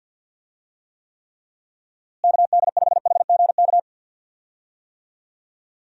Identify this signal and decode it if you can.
Morse code, heard as a keyed tone, about 48 wpm, 700 Hz. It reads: KD5HGK
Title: USA Call Signs - 6 Characters - Repeated 3x - 50wpm